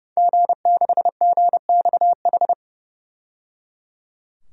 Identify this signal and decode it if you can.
Morse code, heard as a keyed tone, about 30 wpm, 700 Hz. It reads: G6GXH
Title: DXCC-100 Call Signs - 5 Characters - 30wpm